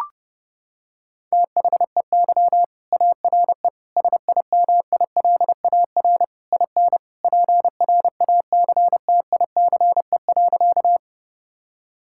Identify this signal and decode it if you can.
Morse code, heard as a keyed tone, about 32 wpm, 700 Hz. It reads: THEY ARE SIMILAR IN PRACTICE.
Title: Sentences from Top 1000 Words - Review Part 5 - 30wpm